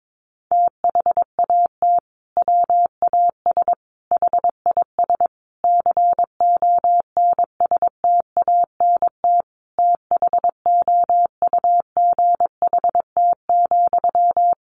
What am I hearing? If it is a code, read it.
Morse code, heard as a keyed tone, about 22 wpm, 700 Hz. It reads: THAT WAS HIS CONSTANT THOUGHT,
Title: Sentences from Top 1500 Words 22wpm